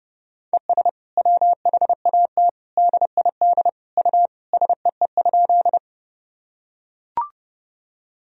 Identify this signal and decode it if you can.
Morse code, heard as a keyed tone, about 32 wpm, 700 Hz. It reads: ES WHAT DID U SEE?
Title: Sentences from Top 100 Words and Abbreviations 30wpm